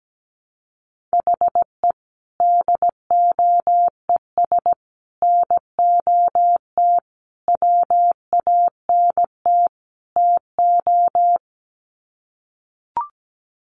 Morse code, heard as a keyed tone, about 17 wpm, 700 Hz. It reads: HE DOES NOT WANT TO
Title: Sets of 5 Words 17wpm